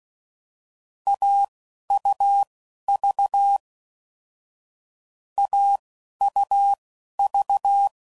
Morse code, heard as a keyed tone, about 16 wpm, spaced as for 11 wpm, 800 Hz. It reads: AUV AUV
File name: Combinations / Difficult Combos / Difficult Combos